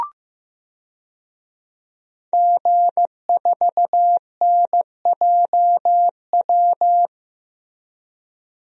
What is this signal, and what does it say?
Morse code, heard as a keyed tone, about 15 wpm, 700 Hz. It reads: G4NJW
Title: DXCC-100 Call Signs - 5 Characters - 15wpm